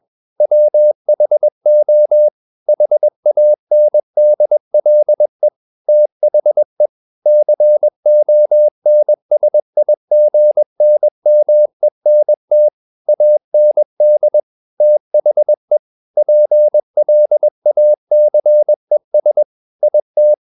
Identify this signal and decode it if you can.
Morse code, heard as a keyed tone, about 21 wpm, 600 Hz. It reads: WHO HANDLE THE CONSIGNMENT AND THE PLACES IT
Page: CW med Gnister